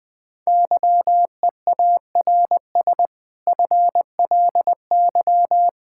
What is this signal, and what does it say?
Morse code, heard as a keyed tone, about 20 wpm, 700 Hz. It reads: YEARS FLY
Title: Difficult Sentences with Hints - 20wpm